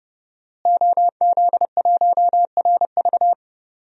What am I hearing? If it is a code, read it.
Morse code, heard as a keyed tone, about 32 wpm, 700 Hz. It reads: OZ1RV
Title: DXCC-100 Call Signs - 5 Characters - Repeated 2x - 30wpm